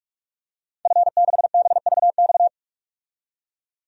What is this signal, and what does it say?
Morse code, heard as a keyed tone, about 41 wpm, 700 Hz. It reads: F6BVX
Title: DXCC-100 Call Signs - 5 Characters - 45wpm